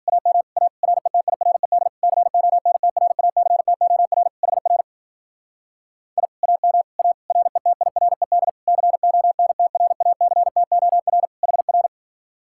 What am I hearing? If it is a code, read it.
Morse code, heard as a keyed tone, about 51 wpm, 700 Hz. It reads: AM A RETIRED CONTRACTOR HR I AM A RETIRED CONTRACTOR HR